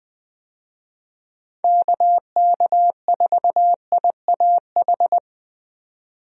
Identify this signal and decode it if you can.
Morse code, heard as a keyed tone, about 20 wpm, 700 Hz. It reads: KK4IAH
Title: USA Call Signs - 6 Characters - 20wpm